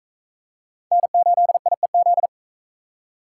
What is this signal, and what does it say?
Morse code, heard as a keyed tone, about 43 wpm, 700 Hz. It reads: N8IEZ